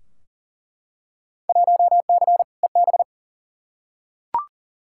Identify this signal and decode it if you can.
Morse code, heard as a keyed tone, about 40 wpm, 700 Hz. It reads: 1C EB
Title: Field Day Exchanges Encore - 40wpm